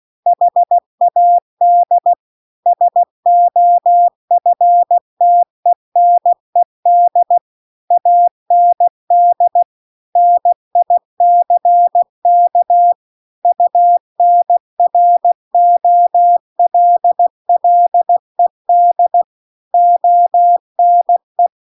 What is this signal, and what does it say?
Morse code, heard as a keyed tone, about 16 wpm, 700 Hz. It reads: HAD SOFTENED AND NICK UNROLLED ONE
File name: CW-The-Big-Twoheaded-River-2-16WPM